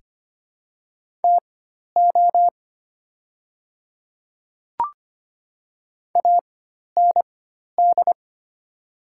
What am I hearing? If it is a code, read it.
Morse code, heard as a keyed tone, about 25 wpm, spaced as for 12 wpm, 700 Hz. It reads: TO AND
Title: 4x Character Spacing - Top 100 Words - 25wpm